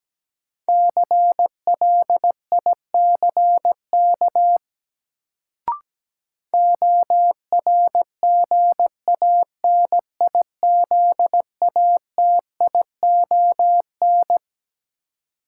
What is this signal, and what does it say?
Morse code, heard as a keyed tone, about 17 wpm, 700 Hz. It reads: CLICK ORGANIZATION
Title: Sentences from Top 1100 Words - New Word Drill Alt - 17wpm